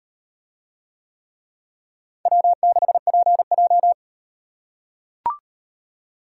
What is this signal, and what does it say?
Morse code, heard as a keyed tone, about 37 wpm, 700 Hz. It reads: W6PJ